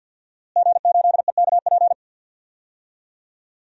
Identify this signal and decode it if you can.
Morse code, heard as a keyed tone, about 49 wpm, 700 Hz. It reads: G8EKP
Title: DXCC-100 Call Signs - 5 Characters - Repeated 3x - 50wpm